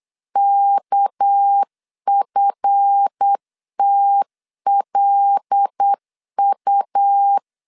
Morse code, heard as a keyed tone, about 8 wpm, 800 Hz. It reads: KFTLU
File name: LEZIONI / 9 2 8wpm 3437